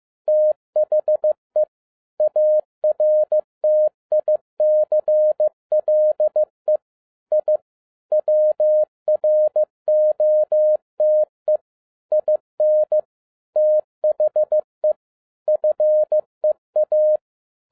Morse code, heard as a keyed tone, about 15 wpm, 610 Hz. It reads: THE ARTICLE I WROTE IN THE FEA
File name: 2012-01-mp3 / n13-A NEWLY DESIGNED QSL CARD-15wpm